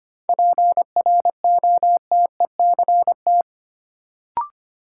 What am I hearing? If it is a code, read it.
Morse code, heard as a keyed tone, about 25 wpm, 700 Hz. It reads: PROTECT
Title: Sentences from Top 900 Words - New Word Drill - 25wpm